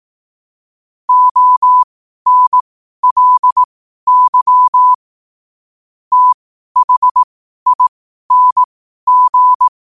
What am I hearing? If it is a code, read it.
Morse code, heard as a keyed tone, about 18 wpm, spaced as for 12 wpm, 1000 Hz. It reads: ONLY THING
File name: qotdfiles / 12wpm